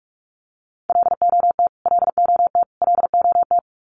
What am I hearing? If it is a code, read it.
Morse code, heard as a keyed tone, about 44 wpm, 700 Hz. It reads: LOT LOT LOT
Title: Top 500 Words Repeated 45wpm